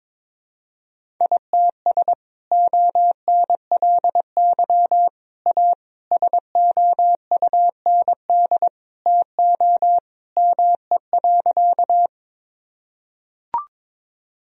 Morse code, heard as a keyed tone, about 22 wpm, 700 Hz. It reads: ITS ONLY A SOUND TO ME.
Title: Sentences from Top 1000 Words - Review Part 1 - 22wpm